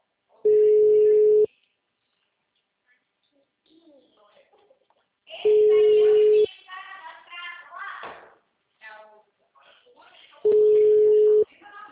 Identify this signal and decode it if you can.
Ringback tone (Europe, 425 Hz), with speech.
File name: out- -2006- - - .73331